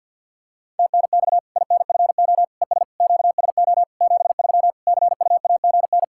Morse code, heard as a keyed tone, about 45 wpm, 700 Hz. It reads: TNX INFO ES QSO 73 CUAGN
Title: Practice QSO - Exchange 4 - Ending - 50wpm